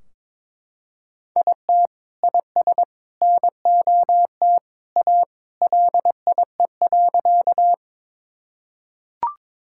Morse code, heard as a keyed tone, about 22 wpm, 700 Hz. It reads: IT IS NOT A LIE.
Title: Sentences from Top 1600 Words 22wpm